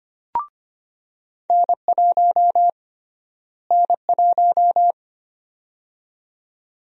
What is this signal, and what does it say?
Morse code, heard as a keyed tone, about 25 wpm, 700 Hz. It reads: N1 N1
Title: USA Call Sign Prefixes - Repeated 2x - 25wpm